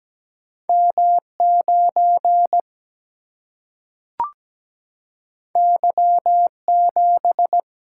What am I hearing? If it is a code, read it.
Morse code, heard as a keyed tone, about 17 wpm, 700 Hz. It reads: M9 Y7
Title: Single-Letter Single-Number 17wpm